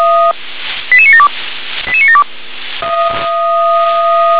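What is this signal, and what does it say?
Five-tone selective call (ZVEI), heard as quick runs of five tones, tones of about 70 ms. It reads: 88972 88972
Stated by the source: ZVEI five-tone selective call received by SDR.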